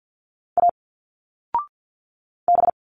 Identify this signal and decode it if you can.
Morse code, heard as a keyed tone, about 50 wpm, 700 Hz. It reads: A B
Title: Single Letters 50wpm